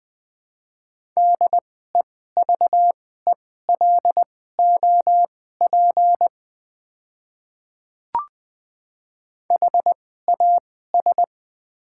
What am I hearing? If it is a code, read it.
Morse code, heard as a keyed tone, about 20 wpm, spaced as for 14 wpm, 700 Hz. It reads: DEVELOP HAS
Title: 2x Character Spacing - Top 500 Words - 20wpm